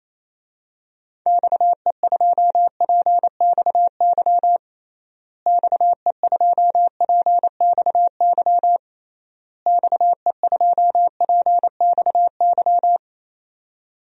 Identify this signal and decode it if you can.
Morse code, heard as a keyed tone, about 28 wpm, 700 Hz. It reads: XE2PXY XE2PXY XE2PXY
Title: DXCC-100 Call Signs - 6 Characters - Repeated 3x - 28wpm